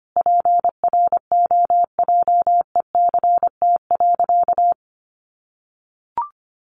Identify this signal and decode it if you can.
Morse code, heard as a keyed tone, about 25 wpm, 700 Hz. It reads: PROJECT.
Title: Sentences from Top 1400 Words 25wpm